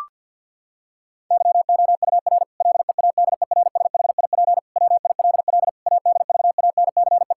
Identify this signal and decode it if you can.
Morse code, heard as a keyed tone, about 50 wpm, 700 Hz. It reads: YOUR LEADERSHIP WILL ADVANCE
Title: Sentences from Top 1600 Words 50wpm